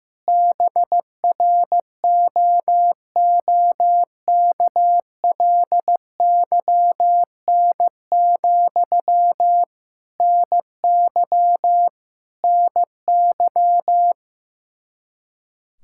Morse code, heard as a keyed tone, about 15 wpm, 700 Hz. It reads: BROOKLYN, NY NY
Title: K1USN SST QSO Practice 15wpm